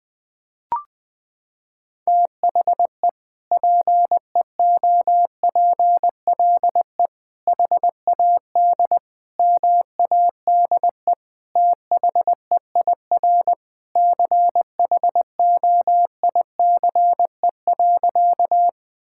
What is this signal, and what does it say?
Morse code, heard as a keyed tone, about 20 wpm, 700 Hz. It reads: THE PEOPLE HAD MADE THEIR CHOICE.
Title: Sentences from Top 1000 Words - Review Part 6 - 20wpm